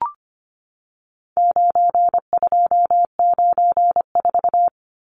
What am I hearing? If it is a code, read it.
Morse code, heard as a keyed tone, about 25 wpm, 700 Hz. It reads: 9294
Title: Numbers - 4-Digit 25wpm